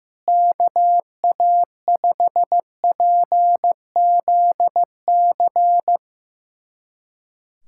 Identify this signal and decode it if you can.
Morse code, heard as a keyed tone, about 15 wpm, 700 Hz. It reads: KA5PZC
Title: USA Callsign DE Callsign - 6 Characters - 15wpm